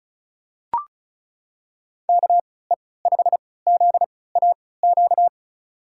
Morse code, heard as a keyed tone, about 36 wpm, spaced as for 20 wpm, 700 Hz. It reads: KE5ZAQ
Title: USA Call Signs - 3x Character Spacing - 35wpm